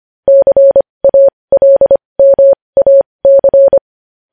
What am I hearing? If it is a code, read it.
Morse code, heard as a keyed tone, about 25 wpm, spaced as for 20 wpm, 550 Hz. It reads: CALMAC
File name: podcast / News headlines-1x-25x20